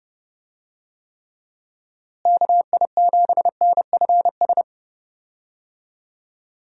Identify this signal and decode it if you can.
Morse code, heard as a keyed tone, about 30 wpm, 700 Hz. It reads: KI7NFS
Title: USA Call Signs - 6 Characters - Repeated 2x - 30wpm